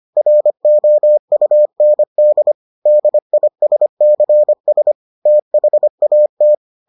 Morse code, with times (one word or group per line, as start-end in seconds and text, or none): 0.2-2.5 ROUND
2.9-4.9 DISCS
5.3-6.5 THAT